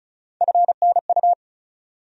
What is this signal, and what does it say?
Morse code, heard as a keyed tone, about 34 wpm, 700 Hz. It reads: FNU
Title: USA Call Sign Postfixes - 3 Letters 35wpm